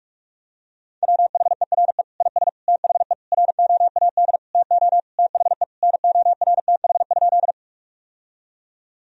Morse code, heard as a keyed tone, about 46 wpm, 700 Hz. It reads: WHERE IS THE ROAD TO THE NORTH?
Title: Sentences from Top 1000 Words - Review Part 2 - 45wpm